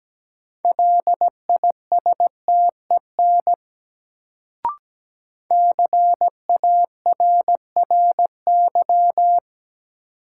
Morse code, heard as a keyed tone, about 17 wpm, 700 Hz. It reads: LISTEN CARRY
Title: Sentences from Top 400 Words - New Word Drill - 17wpm